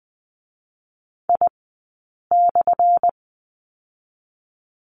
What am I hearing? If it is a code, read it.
Morse code, heard as a keyed tone, about 20 wpm, spaced as for 8 wpm, 700 Hz. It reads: I/
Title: 217 Lesson 35d - Number 0 - Two Characters 20wpm